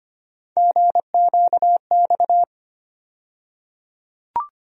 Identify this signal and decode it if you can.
Morse code, heard as a keyed tone, about 25 wpm, 700 Hz. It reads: GQX
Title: USA Call Sign Postfixes - 3 Letters 25wpm